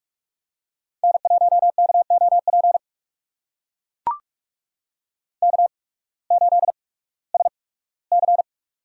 Morse code, heard as a keyed tone, about 44 wpm, spaced as for 21 wpm, 700 Hz. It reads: N1KOP K8SC
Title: USA Call Signs - 8x Character Spacing - 45wpm